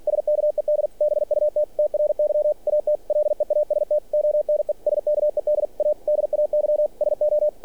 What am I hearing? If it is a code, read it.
Morse code, heard as a keyed tone, about 46 wpm, 600 Hz. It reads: SKED BUT TRY AT LEAST ONE SKED A DAY SO
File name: archived / CRSnet